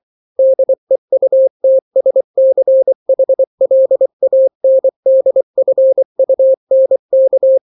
Morse code, heard as a keyed tone, about 24 wpm, 530 Hz. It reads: DEUTSCHLANDFUNK
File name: daily / 23wpm